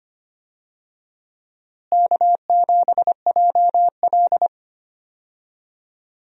Morse code, heard as a keyed tone, about 25 wpm, 700 Hz. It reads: K7JL
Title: USA Call Signs - 4 Characters - Repeated 2x - 25wpm